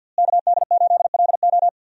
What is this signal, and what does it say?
Morse code, heard as a keyed tone, about 47 wpm, 700 Hz. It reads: KD8LO
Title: USA Call Signs - 5 Characters - Repeated 3x - 50wpm